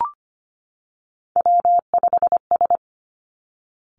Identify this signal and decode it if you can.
Morse code, heard as a keyed tone, about 25 wpm, 700 Hz. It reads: W5S
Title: Three Challenging Letter-Number Combinations 25wpm